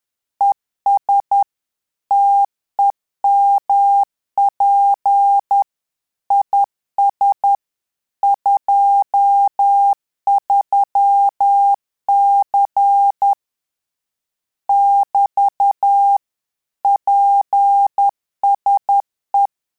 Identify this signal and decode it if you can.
Morse code, heard as a keyed tone, about 10 wpm, 800 Hz. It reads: ES TEMP IS 23C <BT> PSE
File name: Telegraphy-group / QSO CW 10 PPM